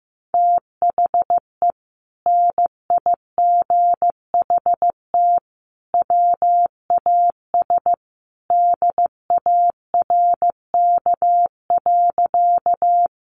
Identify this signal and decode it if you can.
Morse code, heard as a keyed tone, about 15 wpm, 700 Hz. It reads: THE NIGHT WAS DARK.
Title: Sentences from Top 500 Words 15wpm